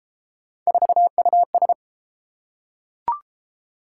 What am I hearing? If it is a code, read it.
Morse code, heard as a keyed tone, about 28 wpm, 700 Hz. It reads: 4US